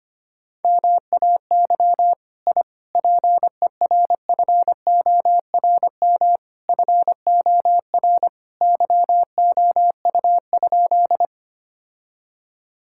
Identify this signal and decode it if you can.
Morse code, heard as a keyed tone, about 25 wpm, 700 Hz. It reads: MAY I PERFORM FOR YOU?
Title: Sentences from Top 1000 Words - Review Part 5 - 25wpm